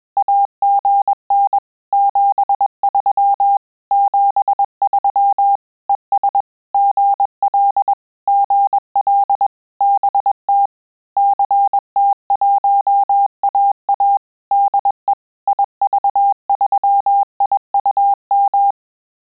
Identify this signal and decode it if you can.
Morse code, heard as a keyed tone, about 21 wpm, 800 Hz. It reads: AGN 73 73 ES GL GL BT CT1AA DE IV3IUM
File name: QSO CW 100 CPM